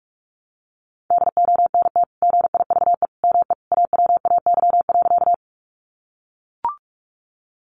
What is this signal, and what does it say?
Morse code, heard as a keyed tone, about 44 wpm, 700 Hz. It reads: DONT GIVE ME AWAY.
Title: Sentences from Top 600 Words 45wpm